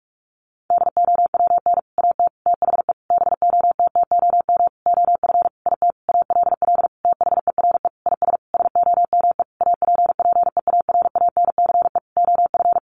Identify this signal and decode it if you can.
Morse code, heard as a keyed tone, about 44 wpm, 700 Hz. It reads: DOWN AT THE BOTTOM OF IT ALL THERE IS SOME APPEARANCE OF
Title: Sentences from Top 1100 Words 45wpm